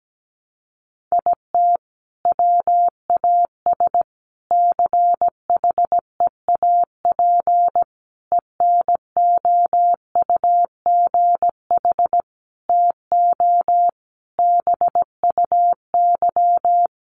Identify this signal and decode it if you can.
Morse code, heard as a keyed tone, about 17 wpm, 700 Hz. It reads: IT WAS CHEAP ENOUGH TO BUY
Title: Sentences from Top 1200 Words 17wpm